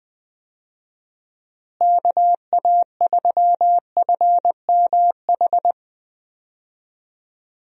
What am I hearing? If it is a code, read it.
Morse code, heard as a keyed tone, about 20 wpm, 700 Hz. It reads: KA3FMH
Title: USA Call Signs - 6 Characters - Repeated 2x - 20wpm